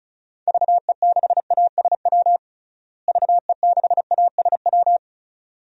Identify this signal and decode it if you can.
Morse code, heard as a keyed tone, about 34 wpm, 700 Hz. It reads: VE6ASW VE6ASW
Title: DXCC-100 Call Signs - 6 Characters - Repeated 2x - 35wpm